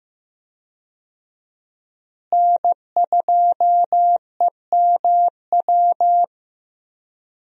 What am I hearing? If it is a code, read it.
Morse code, heard as a keyed tone, about 15 wpm, 700 Hz. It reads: N2EMW
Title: USA Call Signs - 5 Characters - Repeated 2x - 15wpm